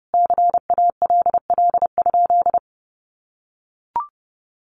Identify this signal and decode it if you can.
Morse code, heard as a keyed tone, about 30 wpm, 700 Hz. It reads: CALL?
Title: Sentences from Top 100 Words 30wpm